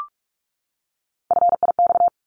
Morse code, heard as a keyed tone, about 44 wpm, 700 Hz. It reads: FIX
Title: ED Suffix - Top 100 Words with Base Words 45wpm